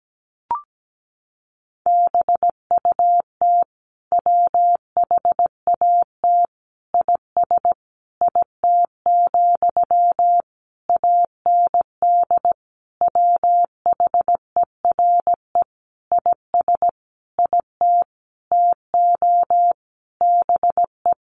Morse code, heard as a keyed tone, about 17 wpm, 700 Hz. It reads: BUT WHAT IS IT, AND WHERE IS IT TO BE
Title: Sentences from Top 300 Words 17wpm